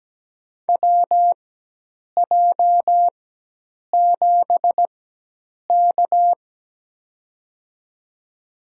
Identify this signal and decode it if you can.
Morse code, heard as a keyed tone, about 17 wpm, spaced as for 8 wpm, 700 Hz. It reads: WJ7K
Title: USA Call Signs - 4 Characters - 4x Character Spacing - 17wpm